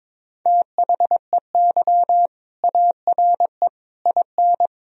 Morse code, heard as a keyed tone, about 21 wpm, 700 Hz. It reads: THEY ARE IN
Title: Sentences from Top 1000 Words - Review Part 3 - 22wpm